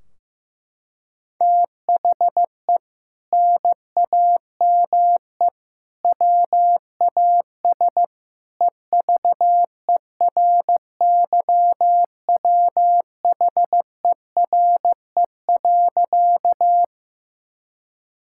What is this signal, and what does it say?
Morse code, heard as a keyed tone, about 15 wpm, 700 Hz. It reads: THE NAME WAS EVERYWHERE.
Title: Sentences from Top 1600 Words 15wpm